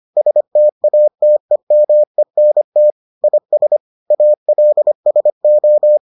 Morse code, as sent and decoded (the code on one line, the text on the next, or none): ... - .- - . -- . -. - / .. ... / .- .-.. ... ---
STATEMENT IS ALSO